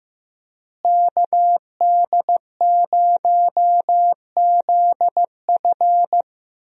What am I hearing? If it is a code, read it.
Morse code, heard as a keyed tone, about 15 wpm, 700 Hz. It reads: KD0ZF
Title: USA Call Signs - 5 Characters - Repeated 3x - 15wpm